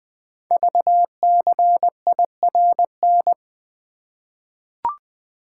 Morse code, heard as a keyed tone, about 20 wpm, 700 Hz. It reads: VCIRN
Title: Code Groups - 5 Letter:Numbers - 20wpm